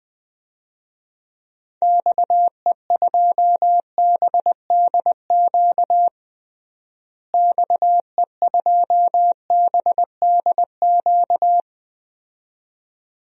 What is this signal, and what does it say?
Morse code, heard as a keyed tone, about 20 wpm, 700 Hz. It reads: XE2BDQ XE2BDQ
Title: DXCC-100 Call Signs - 6 Characters - Repeated 2x - 20wpm